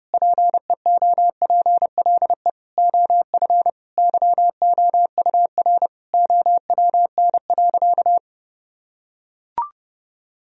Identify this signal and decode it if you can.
Morse code, heard as a keyed tone, about 32 wpm, 700 Hz. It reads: PEOPLE OF YOUR OWN.
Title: Sentences from Top 1000 Words - Review Part 2 - 30wpm